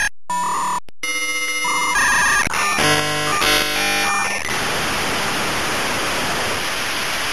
Dial-up modem handshake.
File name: Modem Sounds / sm56 aol k56flex